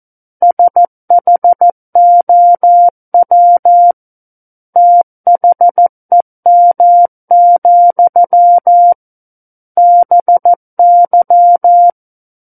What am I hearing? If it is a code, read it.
Morse code, heard as a keyed tone, about 14 wpm, 700 Hz. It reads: SHOW THEM, BY
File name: Book One / 05-Chapter 5 - 14 WPM